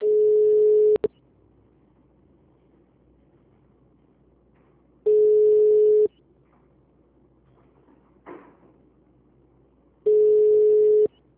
Ringback tone (Europe, 425 Hz).
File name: out- -2006- - - .102